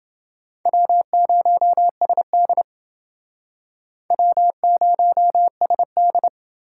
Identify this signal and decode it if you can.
Morse code, heard as a keyed tone, about 29 wpm, 700 Hz. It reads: W0SD W0SD